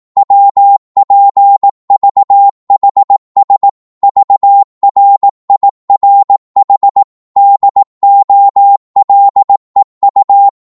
Morse code, heard as a keyed tone, about 18 wpm, 800 Hz. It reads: WPVHS VRIRH DOLEU
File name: Morse / ltr-18wpm